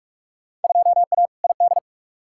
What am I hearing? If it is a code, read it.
Morse code, heard as a keyed tone, about 43 wpm, 700 Hz. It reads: 2A ID
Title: Field Day Exchanges - 45wpm